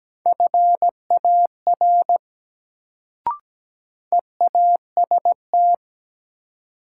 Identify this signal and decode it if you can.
Morse code, heard as a keyed tone, about 17 wpm, 700 Hz. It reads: FAR EAST
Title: Sets of 3 Words - New Word Drill - 17wpm